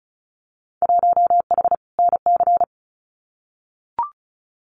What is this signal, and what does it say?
Morse code, heard as a keyed tone, about 34 wpm, 700 Hz. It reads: 1H NC
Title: Winter Field Day Exchanges 35wpm